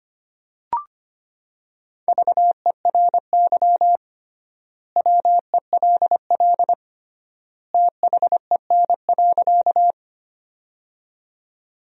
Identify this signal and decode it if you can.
Morse code, heard as a keyed tone, about 25 wpm, 700 Hz. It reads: VERY WELL THEN.
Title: 3x Word-Spacing - Sentences from Top 200 Words - 25wpm